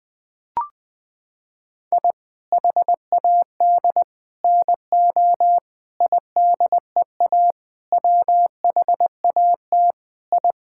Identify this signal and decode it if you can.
Morse code, heard as a keyed tone, about 20 wpm, 700 Hz. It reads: I HAD NO IDEA WHAT I
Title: Sentences with 8-Word Phrases - 20wpm